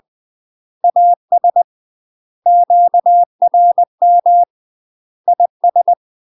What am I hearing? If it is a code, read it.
Morse code, heard as a keyed tone, about 20 wpm, 700 Hz. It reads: AS QRM IS
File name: UK MASTER QSOs TRAINING / 20WPM / UK MASTER QSOs 0069